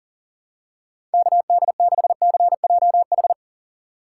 Morse code, heard as a keyed tone, about 40 wpm, 700 Hz. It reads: KD6CJH
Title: USA Call Signs - 6 Characters - 6x Character Spacing - 40wpm